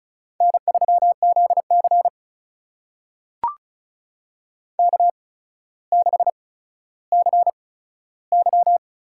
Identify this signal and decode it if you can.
Morse code, heard as a keyed tone, about 34 wpm, spaced as for 15 wpm, 700 Hz. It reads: N3ZC K6CY
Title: USA Call Signs - 4 Characters - 8x Character Spacing - 35wpm